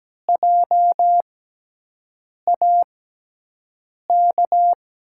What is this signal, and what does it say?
Morse code, heard as a keyed tone, about 17 wpm, 700 Hz. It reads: J A K
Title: USA Call Signs - 6 Characters - 6x Character Spacing - 17wpm